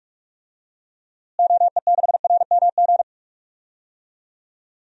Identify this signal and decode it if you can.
Morse code, heard as a keyed tone, about 43 wpm, 700 Hz. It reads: OE6RMG
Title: DXCC-100 Call Signs - 6 Characters - Repeated 2x - 45wpm